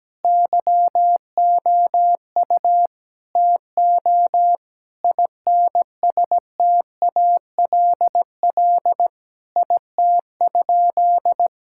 Morse code, heard as a keyed tone, about 17 wpm, 700 Hz. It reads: YOU TO INSTALL IT?
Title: Sentences from Top 1600 Words 17wpm